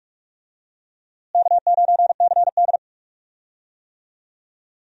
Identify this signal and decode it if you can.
Morse code, heard as a keyed tone, about 42 wpm, 700 Hz. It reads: K9CD
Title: USA Call Signs - 4 Characters - Repeated 2x - 45wpm